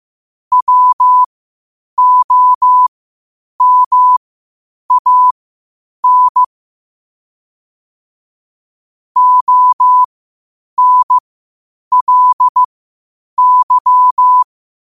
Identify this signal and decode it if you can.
Morse code, heard as a keyed tone, about 15 wpm, spaced as for 8 wpm, 1000 Hz. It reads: WOMAN ONLY
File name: qotdfiles / 8wpm